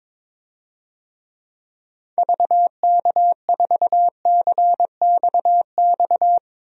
Morse code, heard as a keyed tone, about 21 wpm, 700 Hz. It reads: VK4CXX
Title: DXCC-100 Call Signs - 6 Characters - 22wpm